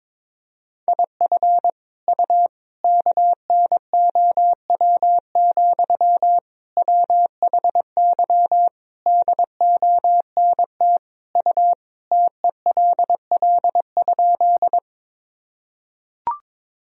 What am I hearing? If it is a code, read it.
Morse code, heard as a keyed tone, about 22 wpm, 700 Hz. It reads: IF U KNOW, WHY DONT U TELL?
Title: Sentences from Top 300 Words and Abbreviations 22wpm